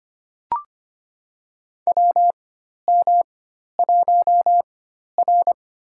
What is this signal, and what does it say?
Morse code, heard as a keyed tone, about 25 wpm, spaced as for 12 wpm, 700 Hz. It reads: WM1R
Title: USA Call Signs - 4 Characters - 4x Character Spacing - 25wpm